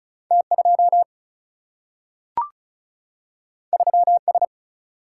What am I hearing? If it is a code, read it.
Morse code, heard as a keyed tone, about 36 wpm, 700 Hz. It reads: T 2 3 S
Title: Single-Letter Single-Number 35wpm